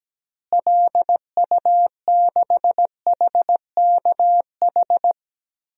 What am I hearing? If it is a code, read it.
Morse code, heard as a keyed tone, about 17 wpm, 700 Hz. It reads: LU6HKH
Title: DXCC-100 Call Signs - 6 Characters - Repeated 3x - 17wpm